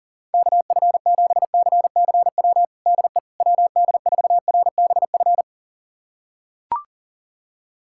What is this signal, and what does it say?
Morse code, heard as a keyed tone, about 37 wpm, 700 Hz. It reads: KF7CCW DE WD4RBF
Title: USA Callsign DE Callsign - 6 Characters - 40wpm